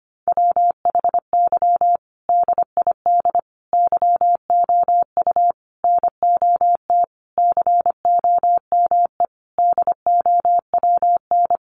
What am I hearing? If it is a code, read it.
Morse code, heard as a keyed tone, about 25 wpm, 700 Hz. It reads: WHY DID YOU NOT COME DOWN
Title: Sentences from Top 1500 Words 25wpm